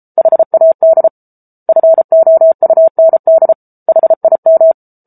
Morse code, heard as a keyed tone, about 34 wpm, 650 Hz. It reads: HAD FOUND HIM
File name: Book Two / 07-Chapter 7 - 33 WPM